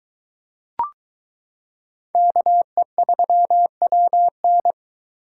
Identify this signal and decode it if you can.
Morse code, heard as a keyed tone, about 23 wpm, 700 Hz. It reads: KE3WN